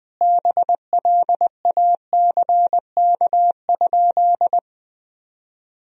Morse code, heard as a keyed tone, about 20 wpm, 700 Hz. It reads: BLACK?
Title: Sentences from Top 400 Words and Abbreviations 20wpm